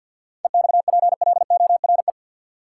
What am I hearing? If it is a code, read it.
Morse code, heard as a keyed tone, about 53 wpm, 700 Hz. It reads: EXPLORE
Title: Sentences from Top 1100 Words - New Word Drill - 50wpm